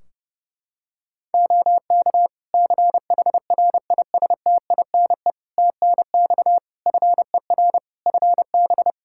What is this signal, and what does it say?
Morse code, heard as a keyed tone, about 30 wpm, 700 Hz. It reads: OK CHRISTINE TNX FER FB
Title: Practice QSO - Exchange 4 - Ending - 30wpm